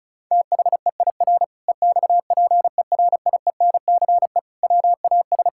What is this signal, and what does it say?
Morse code, heard as a keyed tone, about 36 wpm, 690 Hz. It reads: THEIR EXPERIENCE WAS
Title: Sentences from Top 1000 Words - Review Part 5 - 35wpm